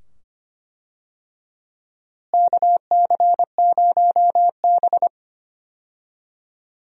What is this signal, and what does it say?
Morse code, heard as a keyed tone, about 25 wpm, 700 Hz. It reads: KC0B
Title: USA Call Signs - 4 Characters - Repeated 2x - 25wpm